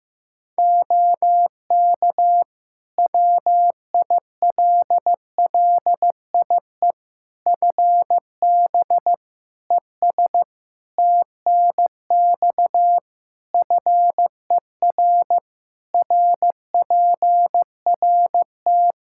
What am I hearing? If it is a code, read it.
Morse code, heard as a keyed tone, about 15 wpm, 700 Hz. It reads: OK WILLIE FB ES TNX FER RPRT
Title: Practice QSO - Exchange 2 - Info - 15wpm